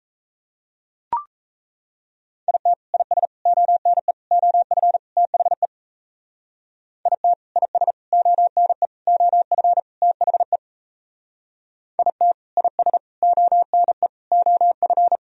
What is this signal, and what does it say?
Morse code, heard as a keyed tone, about 34 wpm, 700 Hz. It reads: IT IS ONE OF THE IT IS ONE OF THE IT IS ONE OF
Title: Speed-Racing - Sets of 5 Words 28wpm